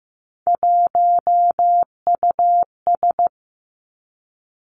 Morse code, heard as a keyed tone, about 15 wpm, 700 Hz. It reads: 1US
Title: Three Challenging Letter-Number Combinations 15wpm